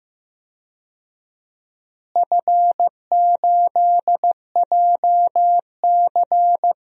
Morse code, heard as a keyed tone, about 15 wpm, 700 Hz. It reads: F8JC
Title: DXCC-100 Call Signs - 4 Characters - Repeated 2x - 15wpm